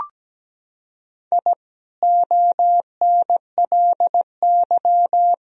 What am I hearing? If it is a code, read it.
Morse code, heard as a keyed tone, about 17 wpm, 700 Hz. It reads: I ONLY
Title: Sentences from Top 1000 Words - Review Part 2 - 17wpm